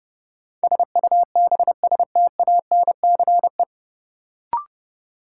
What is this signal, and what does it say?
Morse code, heard as a keyed tone, about 32 wpm, 700 Hz. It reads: SUBSTANCE
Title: Sentences from Top 1500 Words - New Word Drill Alt - 30wpm